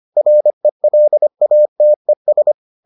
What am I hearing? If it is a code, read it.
Morse code, heard as a keyed tone, about 25 wpm, 600 Hz. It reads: RELATES